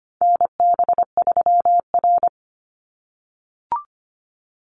Morse code, heard as a keyed tone, about 25 wpm, 700 Hz. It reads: NB3R
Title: Call Signs - 25wpm